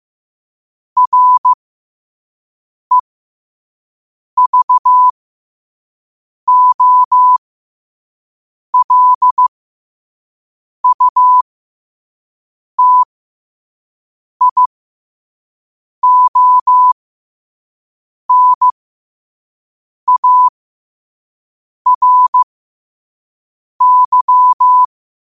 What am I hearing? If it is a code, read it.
Morse code, heard as a keyed tone, about 15 wpm, spaced as for 5 wpm, 1000 Hz. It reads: REVOLUTIONARY
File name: qotdfiles / 5wpm